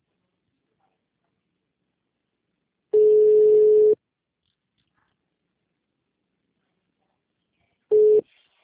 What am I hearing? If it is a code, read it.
Ringback tone (Europe, 425 Hz).
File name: out- -2007- - - .3770